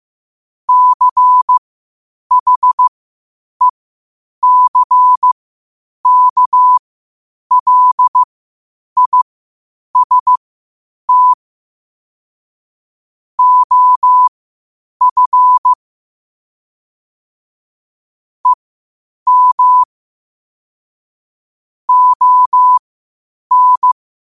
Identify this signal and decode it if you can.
Morse code, heard as a keyed tone, about 15 wpm, spaced as for 8 wpm, 1000 Hz. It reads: CHECKLIST OF EM ON
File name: qotdfiles / 8wpm